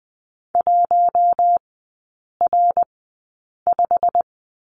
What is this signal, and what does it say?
Morse code, heard as a keyed tone, about 20 wpm, spaced as for 8 wpm, 700 Hz. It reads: 1R5
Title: 121 Lesson 20e - Number 3 - Three Characters 20wpm